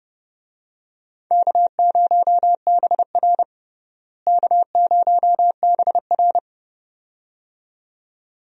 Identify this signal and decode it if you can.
Morse code, heard as a keyed tone, about 29 wpm, 700 Hz. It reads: K0BR K0BR
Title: USA Call Signs - 4 Characters - Repeated 2x - 30wpm